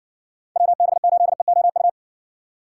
Morse code, heard as a keyed tone, about 60 wpm, 700 Hz. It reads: WB8EOU